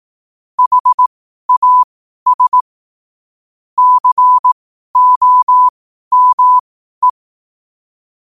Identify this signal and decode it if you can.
Morse code, heard as a keyed tone, about 18 wpm, spaced as for 12 wpm, 1000 Hz. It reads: HAS COME
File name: qotdfiles / 12wpm